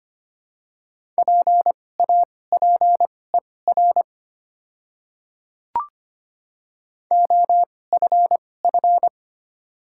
Morse code, heard as a keyed tone, about 25 wpm, spaced as for 18 wpm, 700 Hz. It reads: PAPER OFF
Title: 2x Character Spacing - Top 500 Words - 25wpm